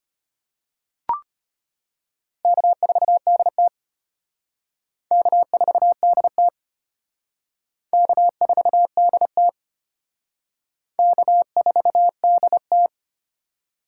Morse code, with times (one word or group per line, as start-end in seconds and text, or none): 2.5-3.7 K4DT
5.1-6.5 K4DT
7.9-9.5 K4DT
11.0-12.9 K4DT